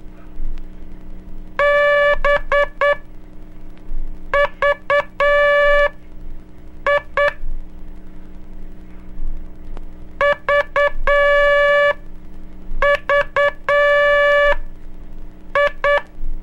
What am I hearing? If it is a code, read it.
Morse code, heard as a keyed tone, about 9 wpm, spaced as for 5 wpm, 580 Hz. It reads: BVI VVI
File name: cursos / Curso CW mp3 / CW Aula - 06